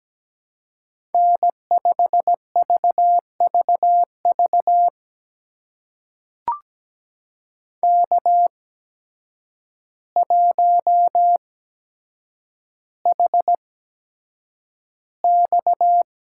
Morse code, heard as a keyed tone, about 17 wpm, spaced as for 8 wpm, 700 Hz. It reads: N5VVV K1HX
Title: USA Call Signs - 8x Character Spacing - 17wpm